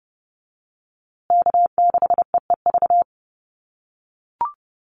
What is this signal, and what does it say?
Morse code, heard as a keyed tone, about 30 wpm, 700 Hz. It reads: K6EEV
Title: USA Call Signs Repeated 2x 30wpm